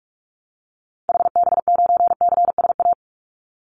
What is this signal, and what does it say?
Morse code, heard as a keyed tone, about 46 wpm, 700 Hz. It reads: HB9CSA
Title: Call Signs - 45wpm